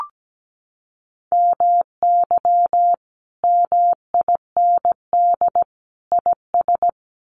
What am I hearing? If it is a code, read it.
Morse code, heard as a keyed tone, about 17 wpm, 700 Hz. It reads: MY MIND IS
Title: Sentences from Top 500 Words 17wpm